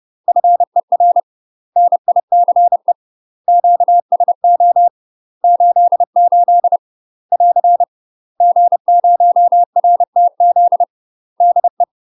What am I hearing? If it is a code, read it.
Morse code, heard as a keyed tone, about 30 wpm, 700 Hz. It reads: FER NICE QSO 88 <AR> G0RTZ DE
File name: UK MASTER QSOs TRAINING / 30WPM / UK MASTER QSOs 0046